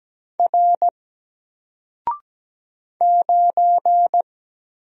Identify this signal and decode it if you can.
Morse code, heard as a keyed tone, about 17 wpm, 700 Hz. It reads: R 9
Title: Single Letter-Number Alt 17wpm